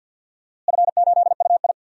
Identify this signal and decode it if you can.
Morse code, heard as a keyed tone, about 53 wpm, 700 Hz. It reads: F8UI
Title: DXCC-100 Call Signs - 4 Characters - Repeated 2x - 50wpm